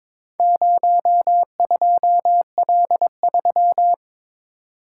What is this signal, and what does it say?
Morse code, heard as a keyed tone, about 22 wpm, 700 Hz. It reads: 02L3
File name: practice / 22wpm / Four Letter-Number Combinations - 22wpm